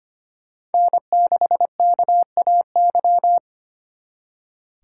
Morse code, heard as a keyed tone, about 29 wpm, 700 Hz. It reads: N6KAY
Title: Speed Racing - USA Call Signs 25wpm